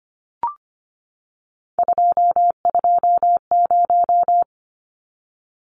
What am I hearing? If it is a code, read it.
Morse code, heard as a keyed tone, about 25 wpm, 700 Hz. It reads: 220
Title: Numbers - 3-Digit 25wpm